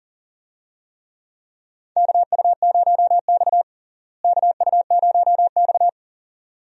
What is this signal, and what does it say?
Morse code, heard as a keyed tone, about 40 wpm, 700 Hz. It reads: KU0X KU0X
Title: USA Call Signs - 4 Characters - Repeated 2x - 40wpm